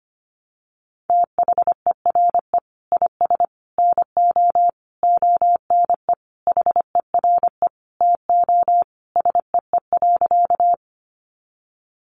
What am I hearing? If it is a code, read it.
Morse code, heard as a keyed tone, about 25 wpm, 700 Hz. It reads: THERE IS NO ONE HERE TO SEE.
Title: Sentences from Top 200 Words 25wpm